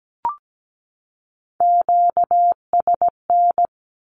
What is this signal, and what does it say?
Morse code, heard as a keyed tone, about 17 wpm, 700 Hz. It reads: QSN
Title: Q-Codes 17wpm